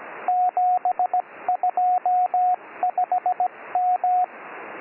Morse code, heard as a keyed tone, about 17 wpm, 730 Hz, with radio noise. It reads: 725M